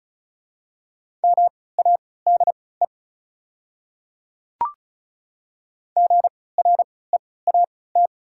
Morse code, heard as a keyed tone, about 36 wpm, spaced as for 20 wpm, 700 Hz. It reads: MADE GREAT
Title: 3x Character Spacing - Top 200 Words - 35wpm